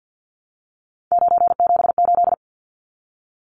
Morse code, heard as a keyed tone, about 50 wpm, 700 Hz. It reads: 978
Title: Numbers - 3-Digit 50wpm